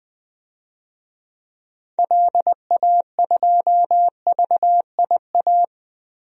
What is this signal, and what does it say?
Morse code, heard as a keyed tone, about 20 wpm, 700 Hz. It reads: LA2VIA
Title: DXCC-100 Call Signs - 6 Characters - Repeated 3x - 20wpm